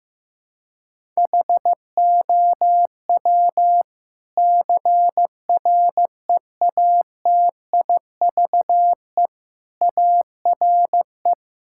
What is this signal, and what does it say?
Morse code, heard as a keyed tone, about 15 wpm, 700 Hz. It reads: HOW CREATIVE ARE
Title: Sentences from Top 1000 Words - Review Part 5 - 15wpm